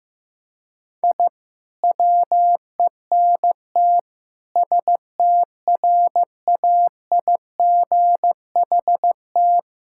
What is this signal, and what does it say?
Morse code, heard as a keyed tone, about 15 wpm, 700 Hz. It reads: I WENT STRAIGHT
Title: Sentences from Top 1000 Words - Review Part 6 - 15wpm